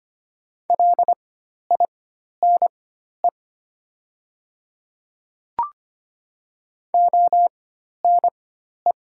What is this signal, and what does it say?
Morse code, heard as a keyed tone, about 25 wpm, spaced as for 12 wpm, 700 Hz. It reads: LINE ONE
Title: 4x Character Spacing - Top 200 Words - 25wpm